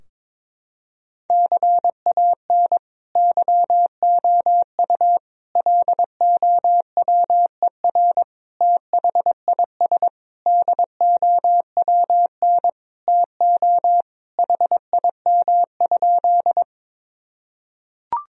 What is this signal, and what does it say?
Morse code, heard as a keyed tone, about 22 wpm, 700 Hz. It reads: CAN YOU LOWER THIS DOWN TO HIM?
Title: Sentences from Top 1000 Words - Review Part 4 - 22wpm